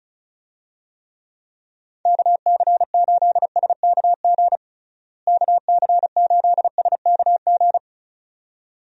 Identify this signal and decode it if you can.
Morse code, heard as a keyed tone, about 36 wpm, 700 Hz. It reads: KC8SKG KC8SKG
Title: USA Call Signs - 6 Characters - Repeated 2x - 35wpm